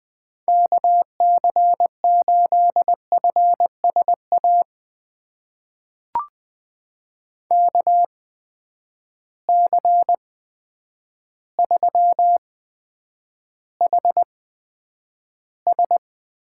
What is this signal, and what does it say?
Morse code, heard as a keyed tone, about 20 wpm, spaced as for 9 wpm, 700 Hz. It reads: KC8FSA KC3HS
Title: USA Call Signs - 6 Characters - 8x Character Spacing - 20wpm